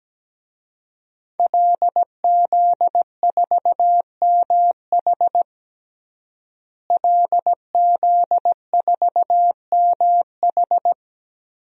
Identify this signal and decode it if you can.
Morse code, heard as a keyed tone, about 17 wpm, 700 Hz. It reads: LZ4MH LZ4MH
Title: DXCC-100 Call Signs - 5 Characters - Repeated 3x - 17wpm